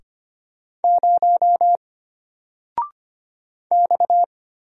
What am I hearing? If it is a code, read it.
Morse code, heard as a keyed tone, about 25 wpm, 700 Hz. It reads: 0 X
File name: Single-Letter-Number-25wpm